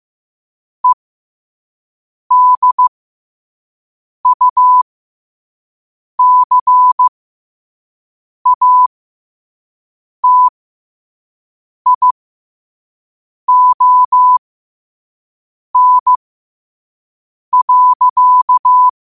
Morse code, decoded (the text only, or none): EDUCATION.